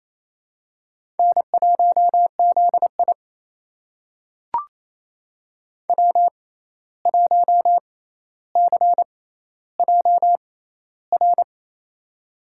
Morse code, heard as a keyed tone, about 28 wpm, spaced as for 13 wpm, 700 Hz. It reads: N1ZI W1CJR
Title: USA Call Signs - 6x Character Spacing - 28wpm